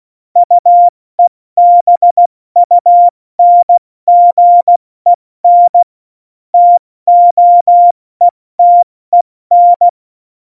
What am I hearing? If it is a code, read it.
Morse code, heard as a keyed tone, about 16 wpm, 700 Hz. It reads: UEBUNGEN TOETEN
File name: Morsekurs / Witze / Witz 19. Dezember 2017, 70 F80